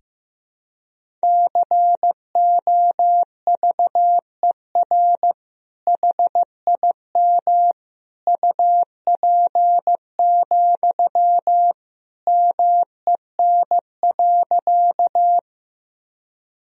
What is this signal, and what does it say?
Morse code, heard as a keyed tone, about 15 wpm, 700 Hz. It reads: COVER HIM UP, MEN.
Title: Sentences from Top 1000 Words - Review Part 1 - 15wpm